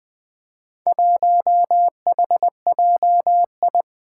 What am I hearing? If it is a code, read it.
Morse code, heard as a keyed tone, about 20 wpm, 700 Hz. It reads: 1HJI
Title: Four Challenging Letter-Number Combinations 20wpm